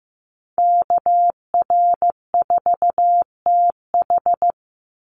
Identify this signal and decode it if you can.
Morse code, heard as a keyed tone, about 15 wpm, 700 Hz. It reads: KR4TH
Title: Call Signs Repeated 3x - 15wpm